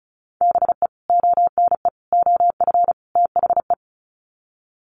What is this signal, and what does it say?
Morse code, heard as a keyed tone, about 34 wpm, 700 Hz. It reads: BE ONE OF THE
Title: Sets of 4 Words 35wpm